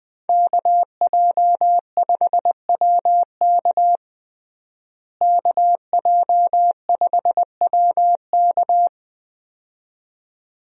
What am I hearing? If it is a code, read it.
Morse code, heard as a keyed tone, about 20 wpm, 700 Hz. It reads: KJ5WK KJ5WK
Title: USA Call Signs - 5 Characters - Repeated 2x - 20wpm